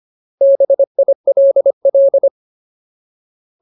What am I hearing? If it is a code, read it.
Morse code, heard as a keyed tone, about 25 wpm, 550 Hz. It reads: BILL
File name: Ultimate Word List / 25WPM Ultimate